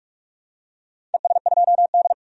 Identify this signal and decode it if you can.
Morse code, heard as a keyed tone, about 41 wpm, 700 Hz. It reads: ES2D
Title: DXCC-100 Call Signs - 4 Characters - 45wpm